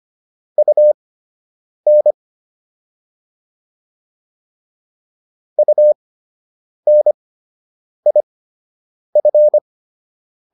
Morse code, heard as a keyed tone, about 25 wpm, spaced as for 8 wpm, 600 Hz. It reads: UN UNIF